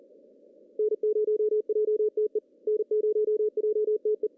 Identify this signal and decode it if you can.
Morse code, heard as a keyed tone, about 38 wpm, 430 Hz. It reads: N0JTE N0JTE
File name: CW / cqww-cw-k5tr